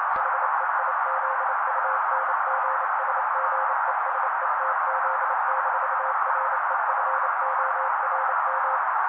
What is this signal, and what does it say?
Morse code, heard as a keyed tone, about 28 wpm, 530 Hz. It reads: HEIZUNGSGESETZ-REFORM
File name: daily / 27wpm noise